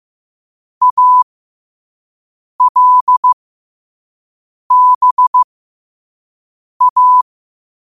Morse code, heard as a keyed tone, about 15 wpm, spaced as for 5 wpm, 1000 Hz. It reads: ALBA
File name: qotdfiles / 5wpm